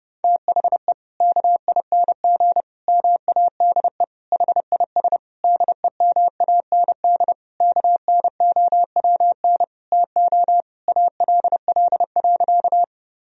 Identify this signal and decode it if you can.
Morse code, heard as a keyed tone, about 30 wpm, 700 Hz. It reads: THE KING MADE HIS DEMAND KNOWN TO ALL.
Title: Sentences from Top 1000 Words - Review Part 4 - 30wpm